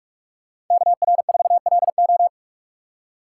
Morse code, heard as a keyed tone, about 44 wpm, 700 Hz. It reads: KR4LO